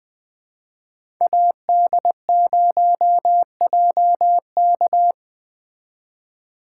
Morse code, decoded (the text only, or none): AD0JK